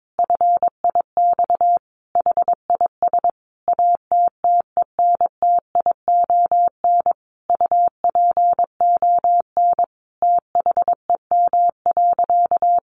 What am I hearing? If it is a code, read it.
Morse code, heard as a keyed tone, about 22 wpm, 700 Hz. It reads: FIX HIS ATTENTION UPON THEM.
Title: Sentences from Top 1400 Words 22wpm